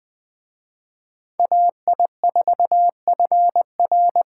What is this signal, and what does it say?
Morse code, heard as a keyed tone, about 24 wpm, 700 Hz. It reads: AI4FR
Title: Speed Racing - USA Call Signs 20wpm